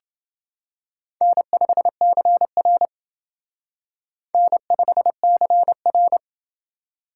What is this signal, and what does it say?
Morse code, heard as a keyed tone, about 28 wpm, 700 Hz. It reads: N5CR N5CR